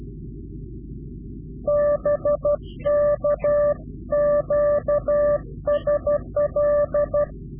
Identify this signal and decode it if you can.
Morse code, heard as a keyed tone, about 12 wpm, 590 Hz. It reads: BK QSL
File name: CW-QSO / Directional Call / Directional QSO